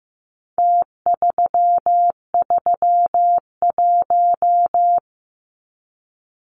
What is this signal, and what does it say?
Morse code, heard as a keyed tone, about 15 wpm, 700 Hz. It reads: T331
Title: Numbers - Contest Style - 4-Digit 15wpm